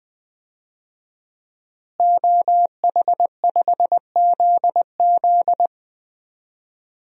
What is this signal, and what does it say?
Morse code, heard as a keyed tone, about 21 wpm, 700 Hz. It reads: OH5ZZ
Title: DXCC-100 Call Signs - 5 Characters - Repeated 3x - 20wpm